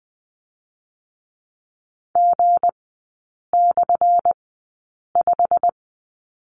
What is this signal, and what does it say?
Morse code, heard as a keyed tone, about 20 wpm, 700 Hz. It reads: G / 5
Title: 178 Lesson 29e - Slash - Three Characters 20wpm